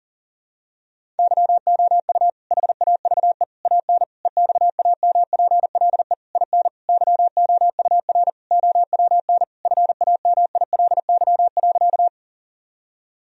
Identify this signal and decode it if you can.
Morse code, heard as a keyed tone, about 37 wpm, 700 Hz. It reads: YOU HAVE AN EXAMPLE IN YOUR OWN FAMILY.
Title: Sentences from Top 1000 Words - Review Part 1 - 40wpm